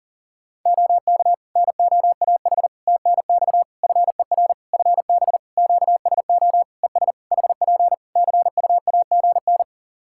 Morse code, heard as a keyed tone, about 40 wpm, 700 Hz. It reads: OK NOAH TNX FER FB QSO ES HP CUAGN
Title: Practice QSO - Exchange 4 - Ending - 40wpm